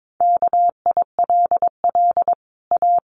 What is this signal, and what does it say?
Morse code, heard as a keyed tone, about 22 wpm, 700 Hz. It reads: KILL A
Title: Sentences from Top 1400 Words 22wpm